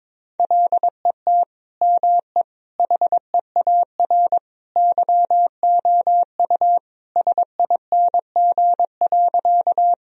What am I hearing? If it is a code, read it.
Morse code, heard as a keyed tone, about 22 wpm, 700 Hz. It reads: LET ME HEAR YOU SING.
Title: Sentences from Top 1000 Words - Review Part 2 - 22wpm